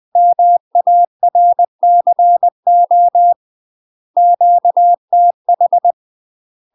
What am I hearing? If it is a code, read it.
Morse code, heard as a keyed tone, about 20 wpm, 700 Hz. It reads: MARCO QTH
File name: UK MASTER QSOs TRAINING / 20WPM / UK MASTER QSOs 0030